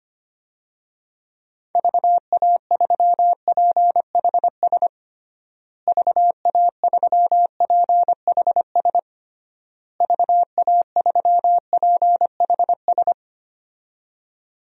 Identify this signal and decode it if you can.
Morse code, heard as a keyed tone, about 25 wpm, 700 Hz. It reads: VA3PHS VA3PHS VA3PHS
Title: DXCC-100 Call Signs - 6 Characters - Repeated 3x - 25wpm